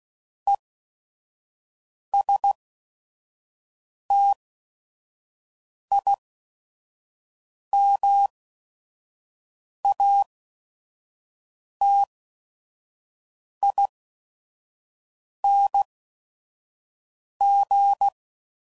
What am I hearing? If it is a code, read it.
Morse code, heard as a keyed tone, about 16 wpm, spaced as for 5 wpm, 800 Hz. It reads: ESTIMATING
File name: Prefixes / By First Letter / Es